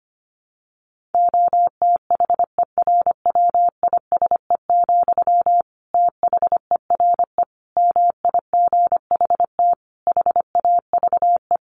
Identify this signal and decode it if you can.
Morse code, heard as a keyed tone, about 25 wpm, 700 Hz. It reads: OTHERWISE, THERE MIGHT HAVE
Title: Sentences from Top 1400 Words 25wpm